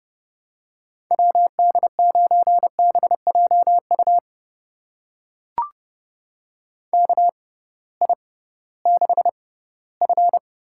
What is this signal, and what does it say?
Morse code, heard as a keyed tone, about 30 wpm, spaced as for 17 wpm, 700 Hz. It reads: WD9BJU KI6F
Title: USA Call Signs - 6 Characters - 6x Character Spacing - 30wpm